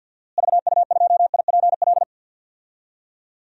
Morse code, heard as a keyed tone, about 51 wpm, 700 Hz. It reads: VU2IPL
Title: DXCC-100 Call Signs - 6 Characters - Repeated 3x - 50wpm